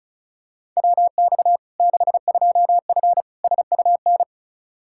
Morse code, heard as a keyed tone, about 36 wpm, 700 Hz. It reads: WX 62F SUN
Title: Practice QSO - Exchange 2 - Info - 35wpm